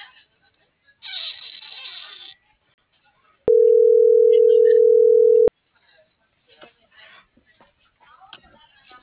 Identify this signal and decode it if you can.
Ringback tone (North America), 440 + 480 Hz.